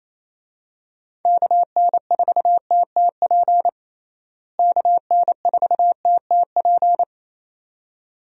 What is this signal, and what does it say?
Morse code, heard as a keyed tone, about 28 wpm, 700 Hz. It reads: KN4TTP KN4TTP
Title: USA Call Signs - 6 Characters - Repeated 2x - 28wpm